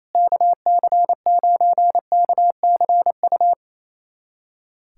Morse code, heard as a keyed tone, about 28 wpm, 700 Hz. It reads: KC9KCU
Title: USA Callsign DE Callsign 28wpm